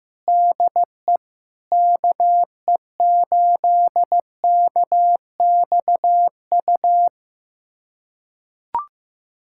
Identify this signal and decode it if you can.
Morse code, heard as a keyed tone, about 15 wpm, 700 Hz. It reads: DE KE8KXU
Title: USA Callsign DE Callsign 15wpm